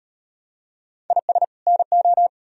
Morse code, heard as a keyed tone, about 38 wpm, 700 Hz. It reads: IS NO